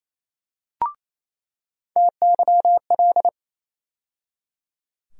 Morse code, heard as a keyed tone, about 28 wpm, 700 Hz. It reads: TYL
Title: USA Call Sign Postfixes - 3 Letters 28wpm